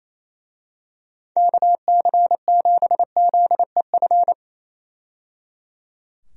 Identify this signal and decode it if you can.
Morse code, heard as a keyed tone, about 28 wpm, 700 Hz. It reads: KC7ZEF
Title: USA Call Signs - 6 Characters - 28wpm